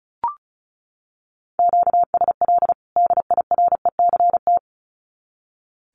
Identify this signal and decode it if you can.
Morse code, heard as a keyed tone, about 34 wpm, 700 Hz. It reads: QSL DIRECT
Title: Common QSO Phrases 35wpm